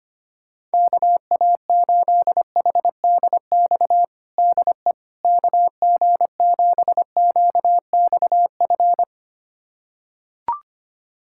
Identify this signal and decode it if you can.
Morse code, heard as a keyed tone, about 25 wpm, 700 Hz. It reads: KA8HDX DE KG7QXF
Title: USA Callsign DE Callsign - 6 Characters - 25wpm